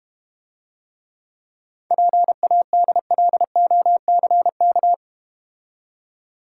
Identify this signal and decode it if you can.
Morse code, heard as a keyed tone, about 30 wpm, 700 Hz. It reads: PADLOCK
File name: Easy Words - Speed-Racing with Hints - 25wpm